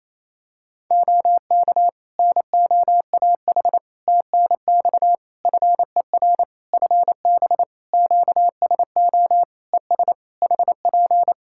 Morse code, heard as a keyed tone, about 27 wpm, 700 Hz. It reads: OK NOAH TNX FER FB QSO ES HP
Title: Practice QSO - Exchange 4 - Ending - 28wpm